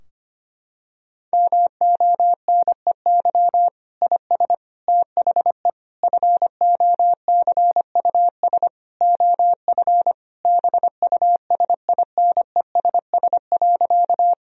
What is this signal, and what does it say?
Morse code, heard as a keyed tone, about 25 wpm, 700 Hz. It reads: MONEY IS THE FOCUS OF BUSINESS.
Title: Sentences from Top 1000 Words - Review Part 4 - 25wpm